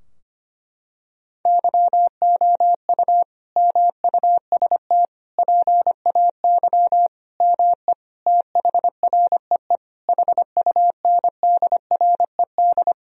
Morse code, heard as a keyed tone, about 25 wpm, 700 Hz. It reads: YOU MUST PAY ME THREE HUNDRED
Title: Sentences from Top 1000 Words - Review Part 2 - 25wpm